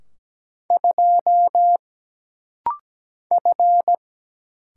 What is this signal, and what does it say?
Morse code, heard as a keyed tone, about 17 wpm, 700 Hz. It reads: 2 F
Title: Single Letter-Number Alt - Rapid-Fire 17wpm